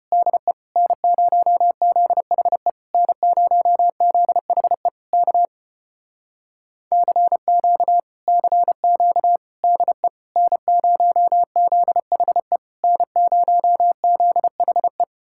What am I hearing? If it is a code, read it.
Morse code, heard as a keyed tone, about 33 wpm, 700 Hz. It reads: DE N0ZHE N0ZHE K CQ CQ DE N0ZHE N0ZHE